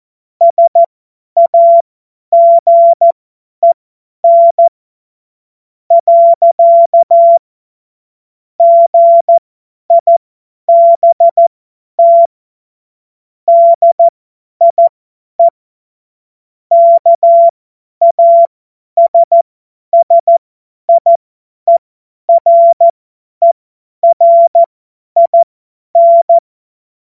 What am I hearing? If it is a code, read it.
Morse code, heard as a keyed tone, about 14 wpm, spaced as for 10 wpm, 680 Hz. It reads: SAGEN. GIBT DIE KASSIERERIN
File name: Morsekurs / Witze / Witz 21.November 2023 50 F70